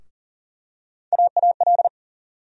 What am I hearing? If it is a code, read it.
Morse code, heard as a keyed tone, about 40 wpm, 700 Hz. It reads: AAL
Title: USA Call Sign Postfixes - 3 Letters 40wpm